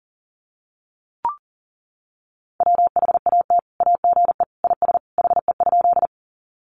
Morse code, heard as a keyed tone, about 40 wpm, 700 Hz. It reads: WHAT AGE IS HE?
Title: Sentences from Top 500 Words 40wpm